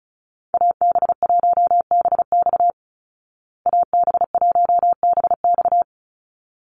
Morse code, heard as a keyed tone, about 34 wpm, 700 Hz. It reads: AB1BX AB1BX
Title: Call Signs Repeated 3x - 35wpm